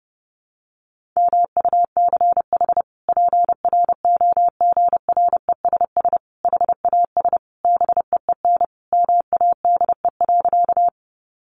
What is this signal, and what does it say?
Morse code, heard as a keyed tone, about 30 wpm, 700 Hz. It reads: MUCH PROGRESS HAS BEEN MADE.
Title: Sentences from Top 1400 Words 30wpm